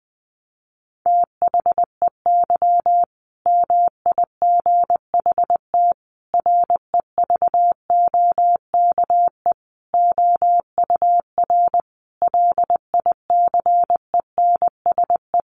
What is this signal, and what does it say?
Morse code, heard as a keyed tone, about 20 wpm, 700 Hz. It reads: THEY MIGHT REVOKE OUR LICENSE
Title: A Sound Of Thunder 20wpm